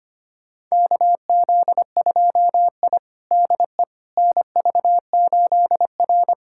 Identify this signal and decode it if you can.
Morse code, heard as a keyed tone, about 25 wpm, 700 Hz. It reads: KZ2I DE NV8R
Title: USA Callsign DE Callsign - 4 Characters - 25wpm